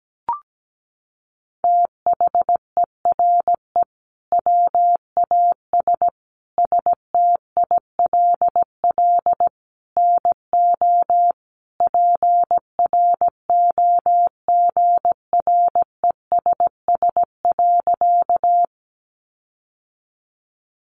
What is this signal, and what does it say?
Morse code, heard as a keyed tone, about 17 wpm, 700 Hz. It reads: THERE WAS STILL NO PROGRESS.
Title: Sentences from Top 1400 Words 17wpm